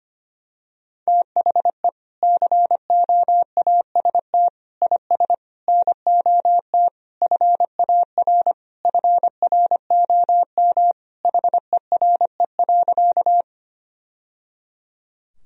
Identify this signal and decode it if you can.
Morse code, heard as a keyed tone, about 25 wpm, 700 Hz. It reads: THE COAST IS NOT FAR FROM HERE.
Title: Sentences from Top 1000 Words - Review Part 4 - 25wpm